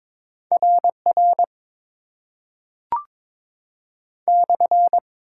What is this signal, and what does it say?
Morse code, heard as a keyed tone, about 22 wpm, 700 Hz. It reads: RR /
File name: Top 90 QSO Elements - 8x Character Spacing - 22wpm